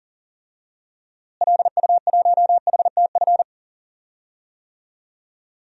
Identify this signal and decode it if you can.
Morse code, heard as a keyed tone, about 40 wpm, 700 Hz. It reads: LU1HTF
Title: DXCC-100 Call Signs - 6 Characters - Repeated 2x - 40wpm